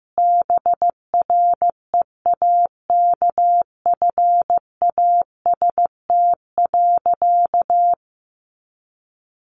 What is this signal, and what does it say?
Morse code, heard as a keyed tone, about 14 wpm, 700 Hz. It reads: BREAKFAST.
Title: Sentences from Top 1500 Words 15wpm